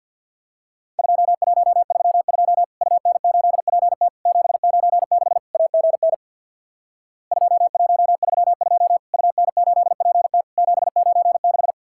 Morse code, heard as a keyed tone, about 50 wpm, 700 Hz. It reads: 2132 UN8PT 796 AGN 2132 UN8PT 796
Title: Worked All Europe DX Contest - QTC Exchange with Repeats - 50wpm